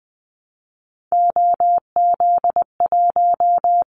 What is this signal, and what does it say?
Morse code, heard as a keyed tone, about 20 wpm, 700 Hz. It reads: OZ1
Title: Callsign Prefixes for Top 50 DXCC - 20wpm